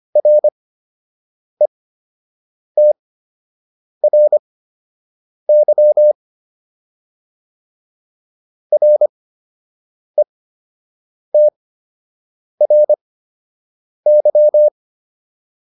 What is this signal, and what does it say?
Morse code, heard as a keyed tone, about 25 wpm, spaced as for 7 wpm, 600 Hz. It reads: RETRY RETRY